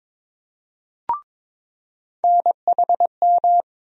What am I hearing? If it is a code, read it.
Morse code, heard as a keyed tone, about 22 wpm, 700 Hz. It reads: NHM
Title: USA Call Sign Postfixes - 3 Letters 22wpm